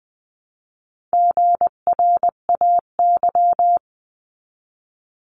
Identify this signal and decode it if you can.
Morse code, heard as a keyed tone, about 20 wpm, 700 Hz. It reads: GRAY
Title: 259 Lesson 42d - Words 18-20wpm